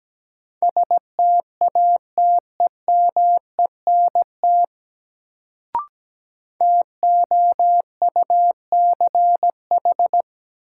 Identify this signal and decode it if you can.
Morse code, heard as a keyed tone, about 17 wpm, 700 Hz. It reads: STATEMENT TOUCH
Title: Sentences from Top 1200 Words - New Word Drill - 17wpm